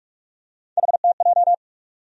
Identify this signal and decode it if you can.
Morse code, heard as a keyed tone, about 46 wpm, 700 Hz. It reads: HTJ
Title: USA Call Sign Postfixes - 3 Letters 45wpm